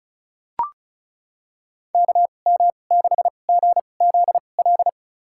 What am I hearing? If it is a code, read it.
Morse code, heard as a keyed tone, about 34 wpm, spaced as for 25 wpm, 700 Hz. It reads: KM6GZL
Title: USA Call Signs - 6 Characters - 2x Character Spacing - 35wpm